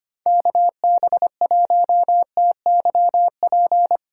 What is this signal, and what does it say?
Morse code, heard as a keyed tone, about 32 wpm, 700 Hz. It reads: KB1TYP